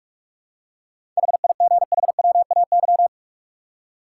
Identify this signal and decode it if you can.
Morse code, heard as a keyed tone, about 46 wpm, 700 Hz. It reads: HIGHWAY
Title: Seven Letter Words - 45wpm